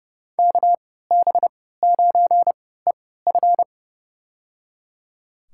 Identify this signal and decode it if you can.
Morse code, heard as a keyed tone, about 30 wpm, spaced as for 17 wpm, 700 Hz. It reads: KB9EF
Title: USA Call Signs - 5 Characters - 3x Character Spacing - 30wpm